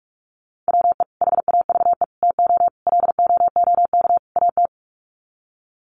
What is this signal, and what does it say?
Morse code, heard as a keyed tone, about 44 wpm, 700 Hz. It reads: WE HAVE TO LOOK AT
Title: Sets of 5 Words Encore2 45wpm